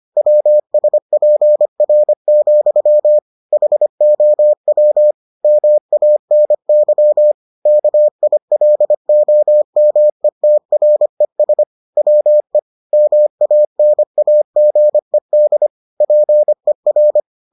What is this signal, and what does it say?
Morse code, heard as a keyed tone, about 25 wpm, 600 Hz. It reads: WSPR, HOW MANY KILOMETRES WE MANAGED PER